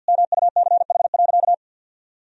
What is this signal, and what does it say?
Morse code, heard as a keyed tone, about 50 wpm, 700 Hz. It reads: MUCH.
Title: Sentences from Top 1000 Words - Review Part 3 - 50wpm